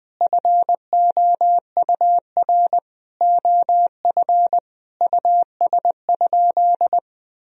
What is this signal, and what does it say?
Morse code, heard as a keyed tone, about 21 wpm, 700 Hz. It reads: FOUR OF US?
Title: Sentences from Top 1000 Words - Review Part 1 - 20wpm